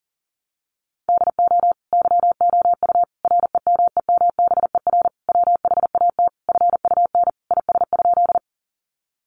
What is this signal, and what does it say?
Morse code, heard as a keyed tone, about 40 wpm, 700 Hz. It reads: DO YOU REMEMBER WHAT FUN IS?
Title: Sentences from Top 700 Words 40wpm